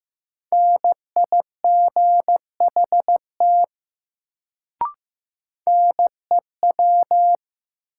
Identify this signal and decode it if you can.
Morse code, heard as a keyed tone, about 15 wpm, 700 Hz. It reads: NIGHT NEW
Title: Sentences from Top 300 Words - New Word Drill - 15wpm